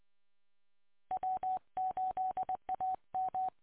Morse code, heard as a keyed tone, about 20 wpm, 750 Hz. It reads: W8AM